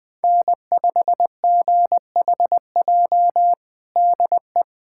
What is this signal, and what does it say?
Morse code, heard as a keyed tone, about 20 wpm, 700 Hz. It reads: N5GHJ DE
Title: USA Callsign DE Callsign - 5 Characters - 20wpm